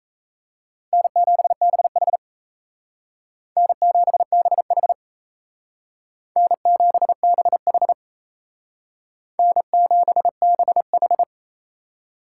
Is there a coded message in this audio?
Yes: Morse code from 0.9 to 11.2 s, heard as a keyed tone, about 36 wpm, 700 Hz. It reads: N7BH N7BH N7BH N7BH